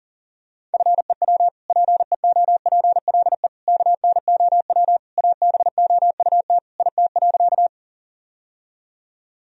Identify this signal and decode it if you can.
Morse code, heard as a keyed tone, about 40 wpm, 700 Hz. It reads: FEW PEOPLE KNOW ABOUT IT.
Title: Sentences from Top 1000 Words - Review Part 1 - 40wpm